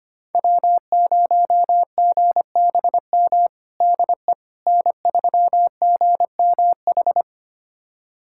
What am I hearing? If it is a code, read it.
Morse code, heard as a keyed tone, about 25 wpm, 700 Hz. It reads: W0GBM DE N3GMH
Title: USA Callsign DE Callsign - 5 Characters - 25wpm